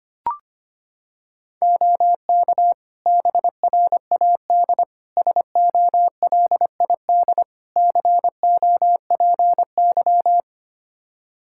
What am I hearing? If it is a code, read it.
Morse code, heard as a keyed tone, about 25 wpm, 700 Hz. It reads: OK BRAD SOLID COPY
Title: K1USN SST QSO Practice 25wpm